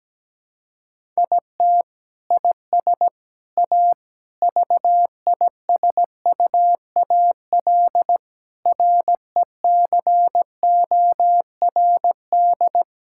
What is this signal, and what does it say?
Morse code, heard as a keyed tone, about 17 wpm, 700 Hz. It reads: IT IS A VISUAL RECORD
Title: Sentences from Top 1000 Words - Review Part 6 - 17wpm